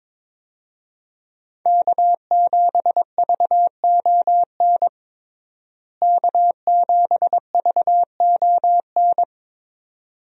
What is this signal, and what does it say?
Morse code, heard as a keyed tone, about 22 wpm, 700 Hz. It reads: K7VON K7VON
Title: USA Call Signs - 5 Characters - Repeated 2x - 22wpm